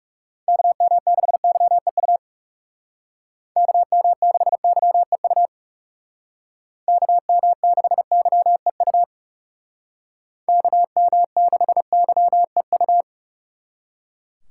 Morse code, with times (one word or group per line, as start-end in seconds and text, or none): 0.5-2.2 KM6YEU
3.6-5.5 KM6YEU
6.9-9.0 KM6YEU
10.5-13.0 KM6YEU